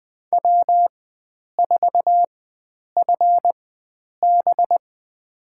Morse code, heard as a keyed tone, about 20 wpm, spaced as for 9 wpm, 700 Hz. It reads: W4FB
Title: USA Call Signs - 4 Characters - 4x Character Spacing - 20wpm